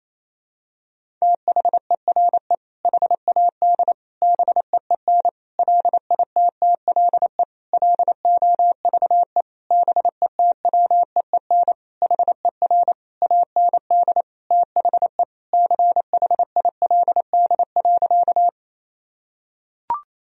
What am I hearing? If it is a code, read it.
Morse code, heard as a keyed tone, about 28 wpm, 700 Hz. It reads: THERE HAD BEEN LITTLE LOVE BETWEEN HER AND THE CHILD.
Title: Sentences from Top 1000 Words - Review Part 1 - 28wpm